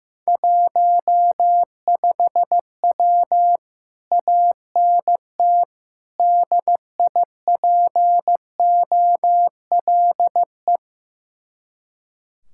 Morse code, heard as a keyed tone, about 15 wpm, 700 Hz. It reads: 15W ANT DIPOLE
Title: Speed-Racing Common QSO Phrases Encore - 15wpm